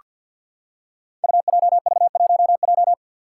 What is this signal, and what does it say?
Morse code, heard as a keyed tone, about 47 wpm, 700 Hz. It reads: UJV1J
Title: Five Challenging Letter-Number Combinations 50wpm